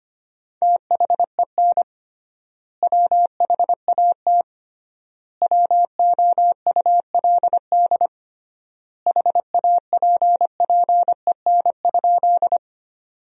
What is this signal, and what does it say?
Morse code, heard as a keyed tone, about 25 wpm, 700 Hz. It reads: THEN WHAT WOULD HAPPEN?
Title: 3x Word-Spacing - Sentences from Top 400 Words - 25wpm